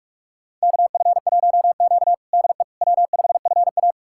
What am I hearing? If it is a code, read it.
Morse code, heard as a keyed tone, about 43 wpm, 700 Hz. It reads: KF1Q DE W5FA
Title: USA Callsign DE Callsign - 4 Characters - 45wpm